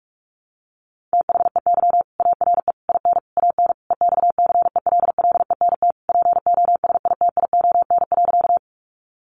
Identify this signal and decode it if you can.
Morse code, heard as a keyed tone, about 44 wpm, 700 Hz. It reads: THEY ARE IN AN EXCELLENT POSITION.
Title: Sentences from Top 1000 Words 45wpm